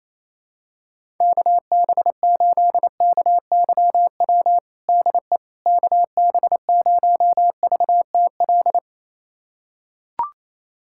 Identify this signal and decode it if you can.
Morse code, heard as a keyed tone, about 28 wpm, 700 Hz. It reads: KB8KYW DE KB0VTL
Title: USA Callsign DE Callsign 28wpm